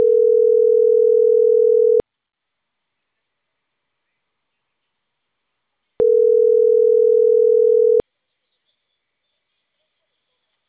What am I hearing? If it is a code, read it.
Ringback tone (North America), 440 + 480 Hz.